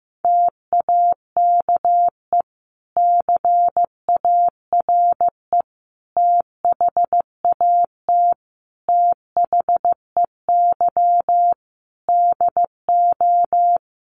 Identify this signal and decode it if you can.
Morse code, heard as a keyed tone, about 15 wpm, 700 Hz. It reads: TAKE CARE THAT THEY DO
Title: Sentences from Top 1500 Words 15wpm